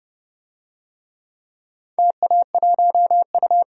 Morse code, heard as a keyed tone, about 30 wpm, 700 Hz. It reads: TA1U
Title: DXCC-100 Call Signs - 4 Characters - 30wpm